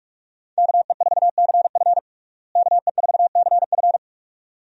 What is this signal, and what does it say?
Morse code, heard as a keyed tone, about 46 wpm, 700 Hz. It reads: KE4CF KE4CF
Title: USA Call Signs - 5 Characters - Repeated 2x - 45wpm